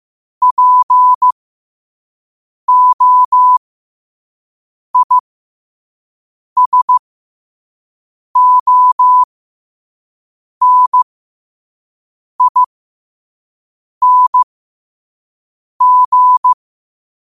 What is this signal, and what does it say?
Morse code, heard as a keyed tone, about 15 wpm, spaced as for 5 wpm, 1000 Hz. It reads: POISONING